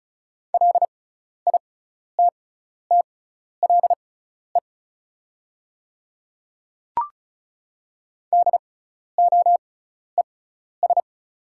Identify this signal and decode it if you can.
Morse code, heard as a keyed tone, about 36 wpm, spaced as for 12 wpm, 690 Hz. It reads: LITTLE DOES
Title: 6x Character Spacing - Top 200 Words - 35wpm